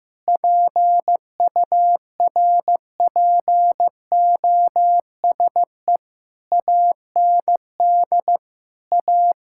Morse code, heard as a keyed tone, about 15 wpm, 700 Hz. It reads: PURPOSE AND A
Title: Sentences from Top 1600 Words 15wpm